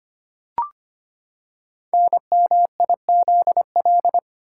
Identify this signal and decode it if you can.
Morse code, heard as a keyed tone, about 25 wpm, 690 Hz. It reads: NMIZL
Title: Code Groups - 5 Letters - 25wpm